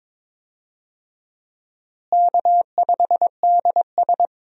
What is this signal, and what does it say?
Morse code, heard as a keyed tone, about 22 wpm, 700 Hz. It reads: K5DS
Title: USA Call Signs - 4 Characters - Repeated 3x - 22wpm